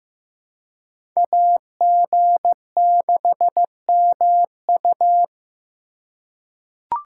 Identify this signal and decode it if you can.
Morse code, heard as a keyed tone, about 15 wpm, 700 Hz. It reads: AG6MU
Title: USA Call Signs - 5 Characters - 2x Character Spacing - 15wpm